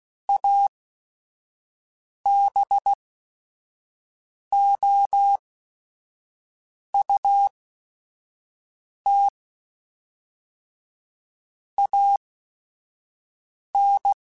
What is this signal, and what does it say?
Morse code, heard as a keyed tone, about 16 wpm, spaced as for 5 wpm, 800 Hz. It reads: ABOUT AN
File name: Technical Subjects / Technical Subjects-mp3s / AND Gate @5